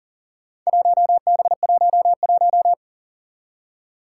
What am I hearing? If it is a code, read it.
Morse code, heard as a keyed tone, about 37 wpm, 700 Hz. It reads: 1B11
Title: Four Challenging Letter-Number Combinations 40wpm